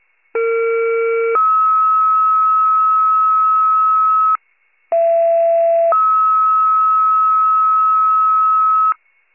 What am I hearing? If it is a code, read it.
Two-tone page, heard as a one-second tone, then a three-second tone, about 460 then 1290 Hz, about 670 then 1290 Hz. It is heard from 0.3 to 8.9 s.
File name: wcalert